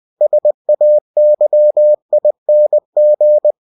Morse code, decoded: SAYING

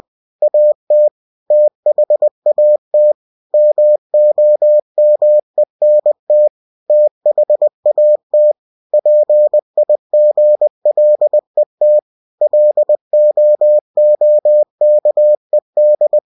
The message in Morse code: AT THAT MOMENT THAT PIGLET LOOKED

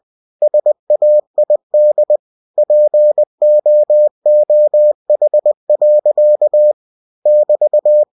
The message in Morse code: SAID POOH. <BT>

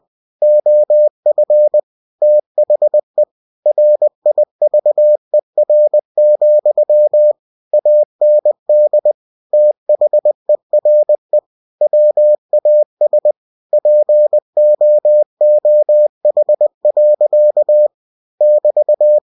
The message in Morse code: OF THE RIVER, AND THERE WAS POOH. <BT>

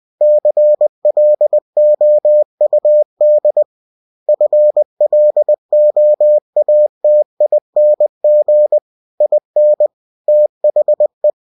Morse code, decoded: CLOUD FLOATING IN THE

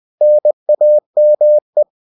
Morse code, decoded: NAME

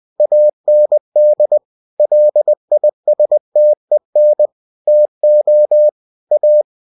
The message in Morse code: AND LISTEN TO A

